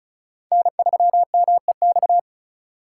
Morse code, heard as a keyed tone, about 36 wpm, 690 Hz. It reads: N3MEX
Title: USA Call Signs - 5 Characters - 2x Character Spacing - 35wpm